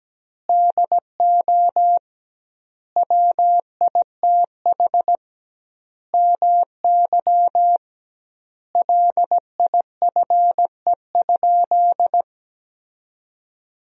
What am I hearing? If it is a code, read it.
Morse code, heard as a keyed tone, about 17 wpm, 700 Hz. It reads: DO WITH MY LIFE?
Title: 2x Word-Spacing - Sentences from Top 300 Words - 17wpm